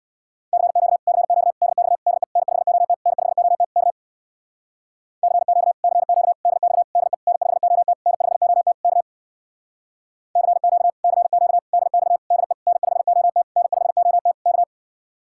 Morse code, heard as a keyed tone, about 61 wpm, 700 Hz. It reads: CQ CQ DX DE N5OT N5OT K CQ CQ DX DE N5OT N5OT K CQ CQ DX DE N5OT N5OT K
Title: Speed-Racing Common QSO Phrases - 50wpm